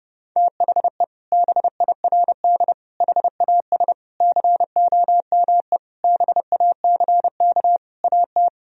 Morse code, heard as a keyed tone, about 30 wpm, 700 Hz. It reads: THE BIRD HAS COME BACK AT
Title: Sentences from Top 1000 Words - Review Part 2 - 30wpm